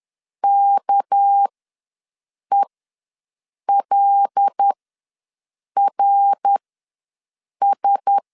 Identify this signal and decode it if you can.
Morse code, heard as a keyed tone, about 10 wpm, spaced as for 6 wpm, 800 Hz. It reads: KELRS
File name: LEZIONI / 6 PKX0 / 6 PK ws 10wpm 3745